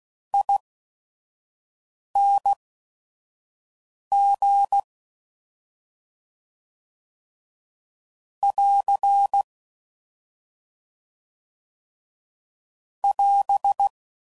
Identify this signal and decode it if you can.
Morse code, heard as a keyed tone, about 16 wpm, spaced as for 5 wpm, 800 Hz. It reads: ING <AR> <AS>
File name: Suffixes / General / ERE INGs